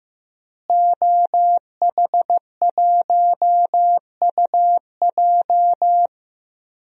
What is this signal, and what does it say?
Morse code, heard as a keyed tone, about 15 wpm, 700 Hz. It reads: OH1UJ
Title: DXCC-100 Call Signs - 5 Characters - Repeated 3x - 15wpm